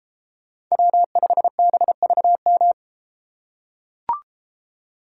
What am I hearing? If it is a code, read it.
Morse code, heard as a keyed tone, about 31 wpm, 700 Hz. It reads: W5BVM